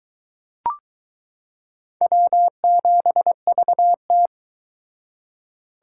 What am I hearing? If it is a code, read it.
Morse code, heard as a keyed tone, about 22 wpm, 700 Hz. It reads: W7VT